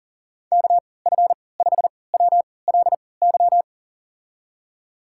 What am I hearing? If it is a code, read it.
Morse code, heard as a keyed tone, about 40 wpm, spaced as for 23 wpm, 700 Hz. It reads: KF5WLY
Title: USA Call Signs - 6 Characters - 3x Character Spacing - 40wpm